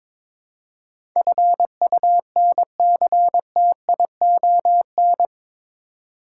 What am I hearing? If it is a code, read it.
Morse code, heard as a keyed tone, about 22 wpm, 700 Hz. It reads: FUNCTION
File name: practice / 22wpm / Eight Letter Words - 22wpm